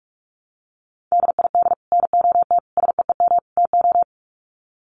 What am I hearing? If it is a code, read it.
Morse code, heard as a keyed tone, about 44 wpm, 700 Hz. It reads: DID NOT SEEM TO
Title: Sets of 4 Words 45wpm